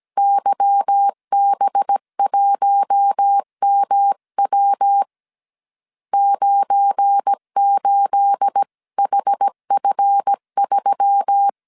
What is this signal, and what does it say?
Morse code, heard as a keyed tone, about 17 wpm, 800 Hz. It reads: YB1MW 98HF3
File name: CW / DET / 1 9 1500C S C 16wpm 2909